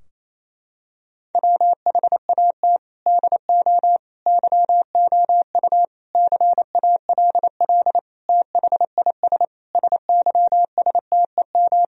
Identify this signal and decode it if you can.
Morse code, heard as a keyed tone, about 28 wpm, 700 Hz. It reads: WHAT DO YOU CALL THIS SYSTEM
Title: Sentences from Top 1000 Words - Review Part 3 - 28wpm